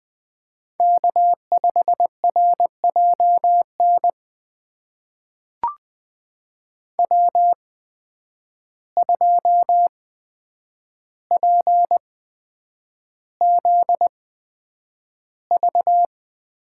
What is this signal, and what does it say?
Morse code, heard as a keyed tone, about 20 wpm, spaced as for 9 wpm, 700 Hz. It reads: K5RJN W2PZV
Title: USA Call Signs - 5 Characters - 8x Character Spacing - 20wpm